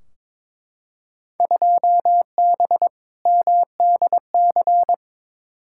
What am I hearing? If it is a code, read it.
Morse code, heard as a keyed tone, about 21 wpm, 700 Hz. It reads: 2B MDC
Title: Field Day Exchanges - 22wpm